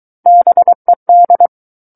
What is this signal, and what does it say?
Morse code, heard as a keyed tone, about 23 wpm, 700 Hz. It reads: BED
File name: Book One / 14-Chapter 14 - 23 WPM